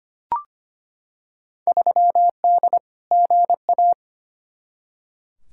Morse code, heard as a keyed tone, about 25 wpm, 700 Hz. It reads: 3D GA
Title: Field Day Exchanges Encore - 25wpm